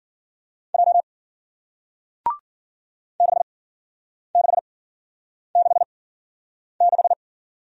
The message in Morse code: J 6666

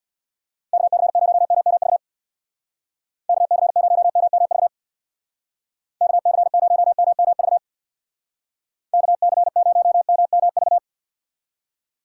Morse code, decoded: KC0MMV KC0MMV KC0MMV KC0MMV

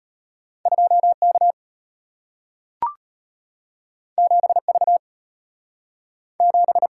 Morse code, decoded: 2K 7V 7

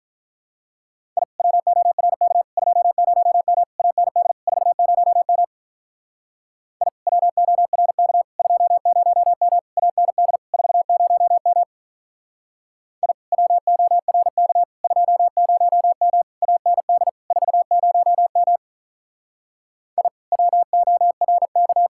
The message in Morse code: I WORK 20M AND 40M I WORK 20M AND 40M I WORK 20M AND 40M I WORK

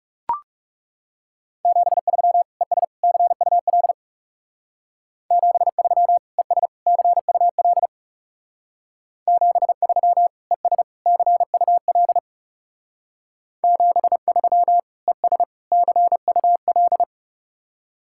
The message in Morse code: 73 ES CUL 73 ES CUL 73 ES CUL 73 ES CUL

